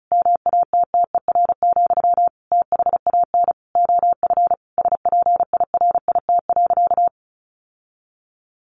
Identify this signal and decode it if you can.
Morse code, heard as a keyed tone, about 34 wpm, 700 Hz. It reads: MATTER, THAN OF SPIRIT.
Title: Sentences from Top 1400 Words 35wpm